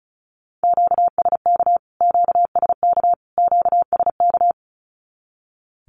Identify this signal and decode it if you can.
Morse code, heard as a keyed tone, about 34 wpm, 700 Hz. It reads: QSK QSK QSK
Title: Q-Codes Repeated 3x 35wpm